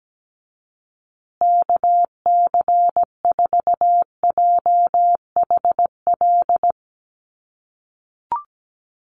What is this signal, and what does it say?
Morse code, heard as a keyed tone, about 17 wpm, 700 Hz. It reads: KC4JHL
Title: USA Call Signs Repeated 2x 17wpm